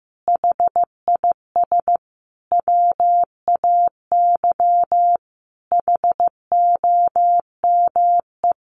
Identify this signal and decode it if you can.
Morse code, heard as a keyed tone, about 15 wpm, 700 Hz. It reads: HIS WAY HOME
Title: Sentences from Top 1400 Words 15wpm